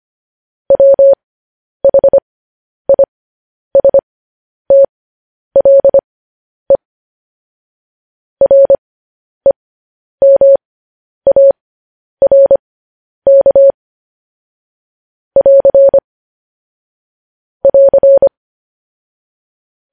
Morse code, heard as a keyed tone, about 25 wpm, spaced as for 10 wpm, 550 Hz. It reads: WHISTLE REMARK <AR> <AR>
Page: News Headlines in Morse Code at 10 WPM